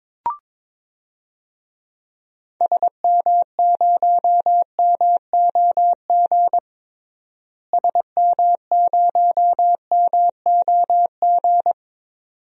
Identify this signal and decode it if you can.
Morse code, heard as a keyed tone, about 22 wpm, 700 Hz. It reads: SM0MOG SM0MOG
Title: DXCC-100 Call Signs - 6 Characters - Repeated 2x - 22wpm